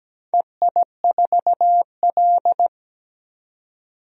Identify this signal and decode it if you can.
Morse code, heard as a keyed tone, about 17 wpm, 700 Hz. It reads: EI4L
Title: DXCC-100 Call Signs - 4 Characters - 17wpm